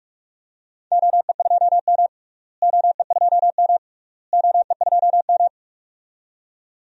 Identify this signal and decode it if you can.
Morse code, heard as a keyed tone, about 47 wpm, 700 Hz. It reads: OE2M OE2M OE2M
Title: DXCC-100 Call Signs - 4 Characters - Repeated 3x - 45wpm